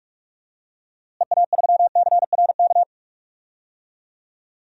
Morse code, heard as a keyed tone, about 46 wpm, 700 Hz. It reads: EA3CRK
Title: DXCC-100 Call Signs - 6 Characters - Repeated 2x - 45wpm